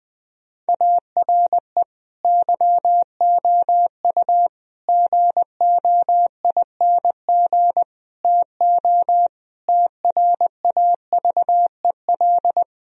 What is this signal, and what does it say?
Morse code, heard as a keyed tone, about 21 wpm, 700 Hz. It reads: ARE YOU GOING TO TRAVEL
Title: Sentences from Top 1000 Words - Review Part 3 - 20wpm